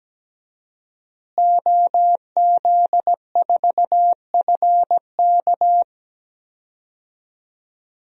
Morse code, heard as a keyed tone, about 17 wpm, 700 Hz. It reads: OZ4FK
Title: DXCC-100 Call Signs - 5 Characters - Repeated 2x - 17wpm